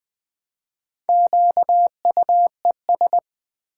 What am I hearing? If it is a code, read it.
Morse code, heard as a keyed tone, about 20 wpm, 700 Hz. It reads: QUES
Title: Build A Word - 4 Letters At A Time - 8-Letter Words - 20wpm